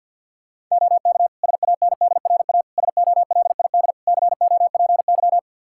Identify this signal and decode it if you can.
Morse code, heard as a keyed tone, about 52 wpm, 700 Hz. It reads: OK SANDRA SOLID COPY
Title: Practice QSO - Exchange 3 - More Info - 50wpm